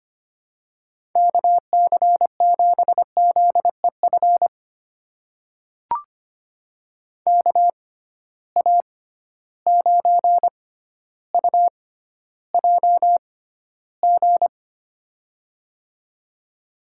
Morse code, heard as a keyed tone, about 25 wpm, spaced as for 13 wpm, 700 Hz. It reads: KC7ZEF KA9UJG
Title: USA Call Signs - 6 Characters - 6x Character Spacing - 25wpm